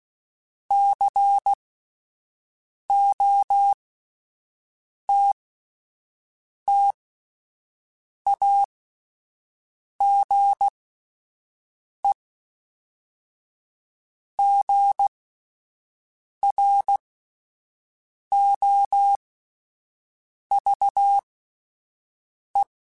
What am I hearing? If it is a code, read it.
Morse code, heard as a keyed tone, about 16 wpm, spaced as for 5 wpm, 800 Hz. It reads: COTTAGEGROVE